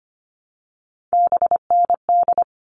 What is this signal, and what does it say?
Morse code, heard as a keyed tone, about 25 wpm, 700 Hz. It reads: BND
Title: All Cut Numbers - 3-Digit 25wpm